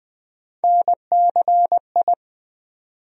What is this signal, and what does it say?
Morse code, heard as a keyed tone, about 20 wpm, 700 Hz. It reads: NCI
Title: USA Call Sign Postfixes - 3 Letters 20wpm